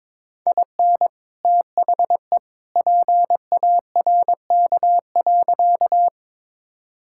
Morse code, heard as a keyed tone, about 22 wpm, 700 Hz. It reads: IN THE PARK.
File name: practice / 22wpm / Sentences with 2-Word Phrases - 22wpm